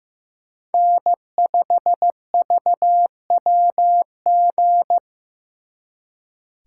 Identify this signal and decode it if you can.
Morse code, heard as a keyed tone, about 18 wpm, 700 Hz. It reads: N5VWG
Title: Speed Racing - USA Call Signs 15wpm